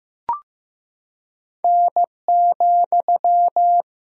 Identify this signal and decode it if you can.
Morse code, heard as a keyed tone, about 16 wpm, 700 Hz. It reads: N,
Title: Sentences from Top 200 Words and Abbreviations 15wpm